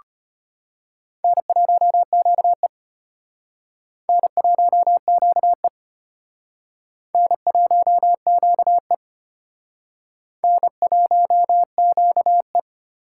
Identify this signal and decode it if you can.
Morse code, heard as a keyed tone, about 32 wpm, 700 Hz. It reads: N1QE N1QE N1QE N1QE